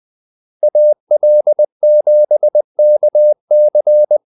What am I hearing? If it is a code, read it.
Morse code, heard as a keyed tone, about 20 wpm, 600 Hz. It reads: AL7KC
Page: Kaldesignaler 20 wpm